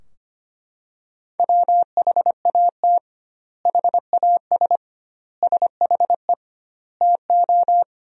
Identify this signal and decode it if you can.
Morse code, heard as a keyed tone, about 25 wpm, 700 Hz. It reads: WHAT HAS SHE TO
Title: 2x Word-Spacing - Sentences from Top 100 Words - 25wpm